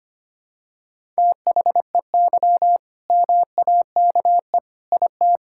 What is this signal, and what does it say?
Morse code, heard as a keyed tone, about 25 wpm, 700 Hz. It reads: THEY MAKE IT
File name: practice / 25wpm / All-Age Appropriate Jokes - 25wpm